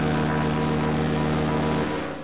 Morse code, heard as a keyed tone, about 18 wpm, 980 Hz. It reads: PER
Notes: This is a sound that was electronically placed onboard the Voyager 1 and 2 spacecraft.